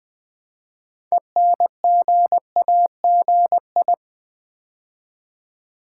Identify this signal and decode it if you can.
Morse code, heard as a keyed tone, about 20 wpm, 700 Hz. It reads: ENGAGI
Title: Build A Word - 2 Letters At A Time - 8-Letter Words - 20wpm